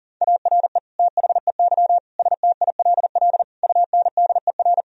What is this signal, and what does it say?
Morse code, heard as a keyed tone, about 40 wpm, 700 Hz. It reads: ARE THEY STILL UNDER
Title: Sentences from Top 1000 Words - Review Part 1 - 40wpm